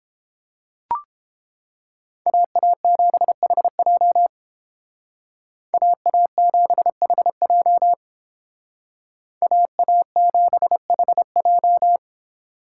Morse code, heard as a keyed tone, about 31 wpm, 700 Hz. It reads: AA7HJ AA7HJ AA7HJ